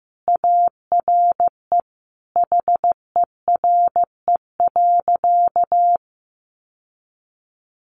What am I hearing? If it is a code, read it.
Morse code, heard as a keyed tone, about 15 wpm, 700 Hz. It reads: ARE HERE.
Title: Sentences from Top 1200 Words 15wpm